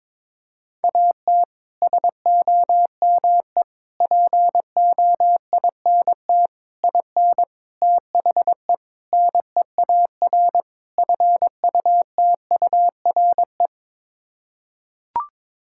Morse code, heard as a keyed tone, about 22 wpm, 700 Hz. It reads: AT SOME POINT IN THE NEAR FUTURE
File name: practice / 22wpm / Sets of 7 Words - 22wpm